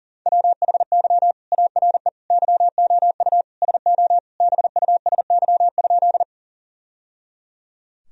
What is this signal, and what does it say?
Morse code, heard as a keyed tone, about 40 wpm, 700 Hz. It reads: WHY ARE YOU SO BUSY?
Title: Sentences from Top 1000 Words - Review Part 3 - 40wpm